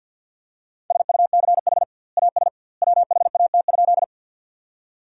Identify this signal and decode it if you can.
Morse code, heard as a keyed tone, about 52 wpm, 700 Hz. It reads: SUCH AS WHAT?
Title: 2x Word-Spacing - Sentences from Top 200 Words - 50wpm